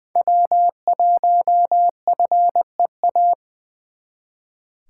Morse code, heard as a keyed tone, about 24 wpm, 700 Hz. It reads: W1FEA
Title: Speed Racing - USA 1x3 Call Signs 20wpm